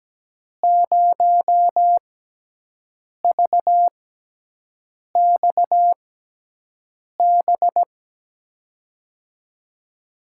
Morse code, heard as a keyed tone, about 17 wpm, spaced as for 6 wpm, 700 Hz. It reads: 0VXB
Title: USA Call Signs - 6 Characters - 6x Character Spacing - 17wpm